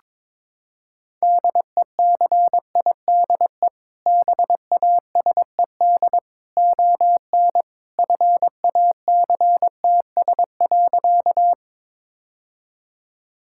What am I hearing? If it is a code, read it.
Morse code, heard as a keyed tone, about 22 wpm, 700 Hz. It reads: DECIDE BASED ON FACTS.
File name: practice / 22wpm / Sentences with 2-Word Phrases - 22wpm